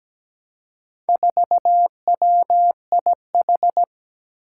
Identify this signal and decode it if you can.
Morse code, heard as a keyed tone, about 17 wpm, 700 Hz. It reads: 4WIH
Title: Four Challenging Letter-Number Combinations 17wpm